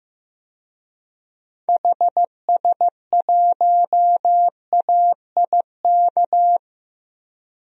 Morse code, heard as a keyed tone, about 15 wpm, 700 Hz. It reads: HS1AIK
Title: DXCC-100 Call Signs - 6 Characters - Repeated 2x - 15wpm